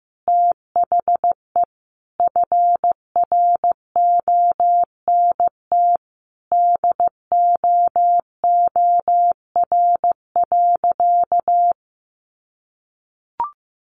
Morse code, heard as a keyed tone, about 15 wpm, 700 Hz. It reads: THE FRONT DOOR.
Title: Sentences from Top 1300 Words 15wpm